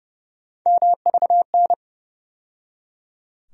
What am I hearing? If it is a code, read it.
Morse code, heard as a keyed tone, about 30 wpm, 700 Hz. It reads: MVN
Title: USA Call Sign Postfixes - 3 Letters 30wpm